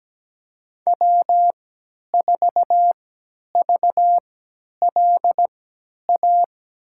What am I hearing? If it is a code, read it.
Morse code, heard as a keyed tone, about 17 wpm, spaced as for 10 wpm, 700 Hz. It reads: W4VLA
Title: USA Call Signs - 5 Characters - 3x Character Spacing - 17wpm